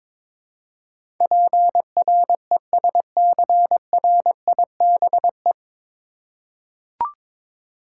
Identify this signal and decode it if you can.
Morse code, heard as a keyed tone, about 22 wpm, 700 Hz. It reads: PRESCRIBE
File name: 9 Letter Words - First 3 Letters Given - 22wpm